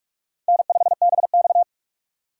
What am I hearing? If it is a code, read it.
Morse code, heard as a keyed tone, about 48 wpm, 700 Hz. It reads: N5BX
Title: USA Call Signs - 4 Characters - 8x Character Spacing - 45wpm